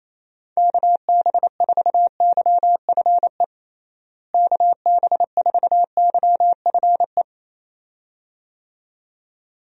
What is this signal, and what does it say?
Morse code, heard as a keyed tone, about 28 wpm, 700 Hz. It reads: KB4YFE KB4YFE
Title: USA Call Signs - 6 Characters - Repeated 2x - 28wpm